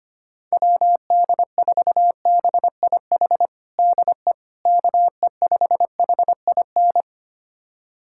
Morse code, heard as a keyed tone, about 25 wpm, 700 Hz. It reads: WD4BIH DE KE5HIN
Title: USA Callsign DE Callsign 25wpm